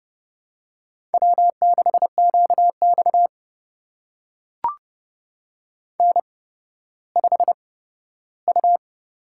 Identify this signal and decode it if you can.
Morse code, heard as a keyed tone, about 32 wpm, spaced as for 15 wpm, 700 Hz. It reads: W6QX N5U
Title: USA Call Signs - 4 Characters - 8x Character Spacing - 30wpm